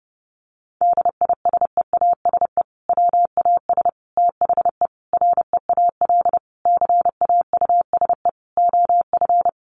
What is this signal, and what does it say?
Morse code, heard as a keyed tone, about 30 wpm, 700 Hz. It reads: DISEASE WAS THE REAL CAUSE OF
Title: Sentences from Top 1100 Words 30wpm